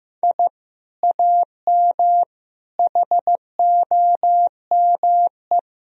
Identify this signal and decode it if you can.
Morse code, heard as a keyed tone, about 15 wpm, 700 Hz. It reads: I AM HOME
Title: Sentences from Top 1000 Words - Review Part 3 - 15wpm